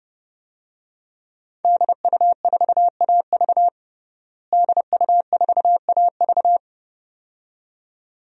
Morse code, heard as a keyed tone, about 30 wpm, 700 Hz. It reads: DU4AV DU4AV
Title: DXCC-100 Call Signs - 5 Characters - Repeated 2x - 30wpm